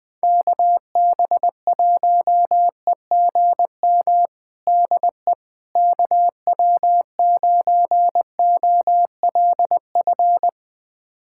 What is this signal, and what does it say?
Morse code, heard as a keyed tone, about 21 wpm, 700 Hz. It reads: KB1EGM DE KW9OLF
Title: USA Callsign DE Callsign 20wpm